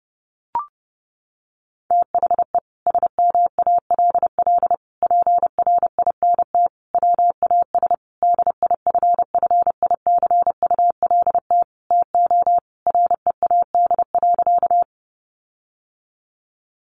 Morse code, heard as a keyed tone, about 30 wpm, 700 Hz. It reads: THE SMALL PRINT WAS DIFFICULT TO READ.
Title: Sentences from Top 1400 Words 30wpm